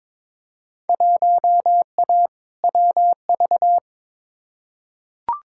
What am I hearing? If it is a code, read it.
Morse code, heard as a keyed tone, about 21 wpm, 700 Hz. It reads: 1A WV
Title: Field Day Exchanges Encore - 22wpm